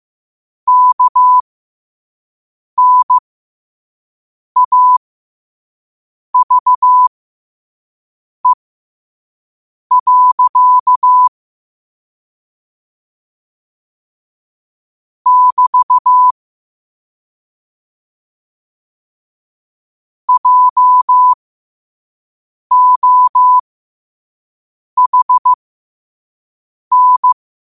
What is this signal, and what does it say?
Morse code, heard as a keyed tone, about 15 wpm, spaced as for 5 wpm, 1000 Hz. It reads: KNAVE. <BT> JOHN
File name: qotdfiles / 5wpm